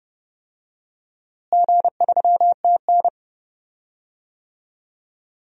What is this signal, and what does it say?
Morse code, heard as a keyed tone, about 30 wpm, 700 Hz. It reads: G3TN
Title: DXCC-100 Call Signs - 4 Characters - Repeated 3x - 30wpm